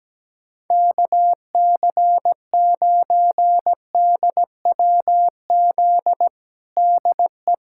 Morse code, heard as a keyed tone, about 17 wpm, 700 Hz. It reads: KC9DWZ DE
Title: USA Callsign DE Callsign - 6 Characters - 17wpm